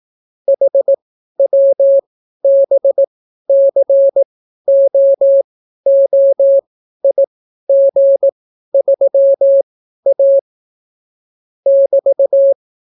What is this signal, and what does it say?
Morse code, heard as a keyed tone, about 18 wpm, spaced as for 12 wpm, 550 Hz. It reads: HWBCOOIG3A <BT>